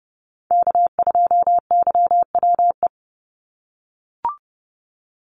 Morse code, heard as a keyed tone, about 30 wpm, 700 Hz. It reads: K2YWE
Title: Call Signs - 30wpm